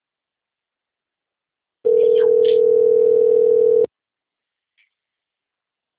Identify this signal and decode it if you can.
Ringback tone (North America), 440 + 480 Hz.
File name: out- - - - - .41199